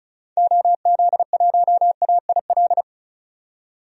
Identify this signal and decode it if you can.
Morse code, heard as a keyed tone, about 36 wpm, 700 Hz. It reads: OZ1AIL
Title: DXCC-100 Call Signs - 6 Characters - 35wpm